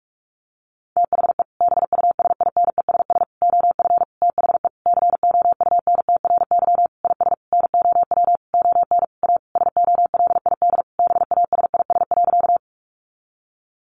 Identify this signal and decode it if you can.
Morse code, heard as a keyed tone, about 44 wpm, 700 Hz. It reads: THE BUSINESS OF THE COUNTRY IS NOW ON A SOLID BASIS.
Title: Sentences from Top 1500 Words 45wpm